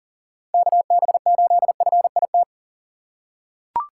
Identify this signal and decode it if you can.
Morse code, heard as a keyed tone, about 40 wpm, 700 Hz. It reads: KB8FIT
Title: USA Call Signs - 6 Characters - 2x Character Spacing - 40wpm